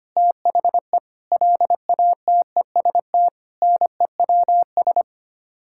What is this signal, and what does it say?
Morse code, heard as a keyed tone, about 25 wpm, 700 Hz. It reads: THE LATEST NEWS
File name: Sets of 9 Words - 25wpm